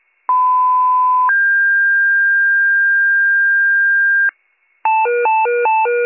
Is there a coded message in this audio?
Yes: two-tone page from 0.3 to 4.3 s, heard as a one-second tone, then a three-second tone, about 1010 then 1600 Hz.